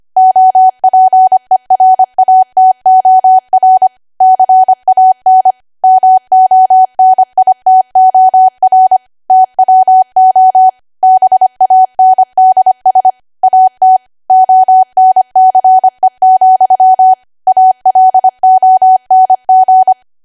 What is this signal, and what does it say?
Morse code, heard as a keyed tone, about 25 wpm, 750 Hz. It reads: OPERATOR CAN MONITOR TWO BANDS AT ONCE, ALONG
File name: Archive / 25 WPM / 25WPM